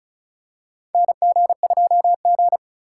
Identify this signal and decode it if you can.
Morse code, heard as a keyed tone, about 36 wpm, 700 Hz. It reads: NG2G
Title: USA Call Signs - 4 Characters - 8x Character Spacing - 35wpm